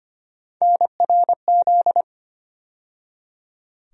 Morse code, heard as a keyed tone, about 25 wpm, 700 Hz. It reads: NRZ
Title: USA Call Sign Postfixes - 3 Letters 25wpm